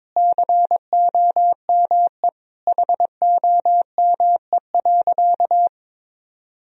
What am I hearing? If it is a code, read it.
Morse code, heard as a keyed tone, about 22 wpm, 700 Hz. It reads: COME HOME.
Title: Sentences from Top 1000 Words - Review Part 3 - 22wpm